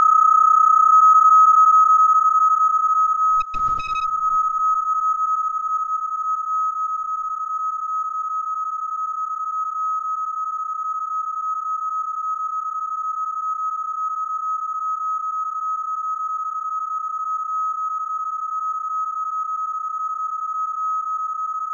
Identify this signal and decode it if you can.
PSK31, about 31.25 baud.
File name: onair/bpsk31a